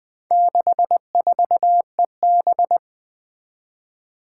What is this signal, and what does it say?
Morse code, heard as a keyed tone, about 20 wpm, 700 Hz. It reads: 64EB
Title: Four Challenging Letter-Number Combinations 20wpm